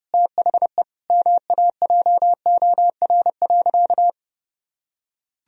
Morse code, heard as a keyed tone, about 30 wpm, 700 Hz. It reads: THE MAJOR.
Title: Sentences from Top 1000 Words - Review Part 4 - 30wpm